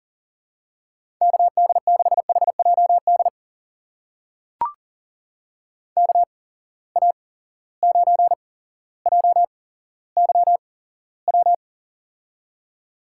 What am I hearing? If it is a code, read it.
Morse code, heard as a keyed tone, about 40 wpm, spaced as for 17 wpm, 700 Hz. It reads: KD6HJD KA9JYW
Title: USA Call Signs - 8x Character Spacing - 40wpm